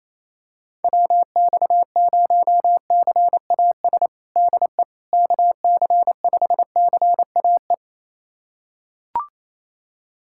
Morse code, heard as a keyed tone, about 28 wpm, 700 Hz. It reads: WX0CAS DE KC5CAE
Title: USA Callsign DE Callsign - 6 Characters - 28wpm